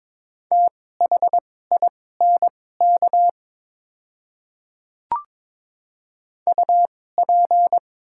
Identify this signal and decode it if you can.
Morse code, heard as a keyed tone, about 22 wpm, spaced as for 16 wpm, 700 Hz. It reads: THINK UP
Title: 2x Character Spacing - Top 200 Words - 22wpm